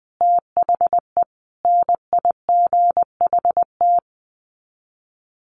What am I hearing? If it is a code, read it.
Morse code, heard as a keyed tone, about 20 wpm, 700 Hz. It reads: THE NIGHT
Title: Sets of 4 Words 20wpm